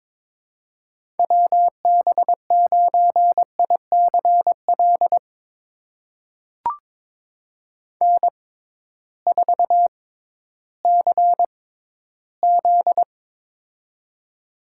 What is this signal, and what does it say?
Morse code, heard as a keyed tone, about 22 wpm, spaced as for 13 wpm, 700 Hz. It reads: WB9ICL N4CZ
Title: USA Call Signs - 6x Character Spacing - 22wpm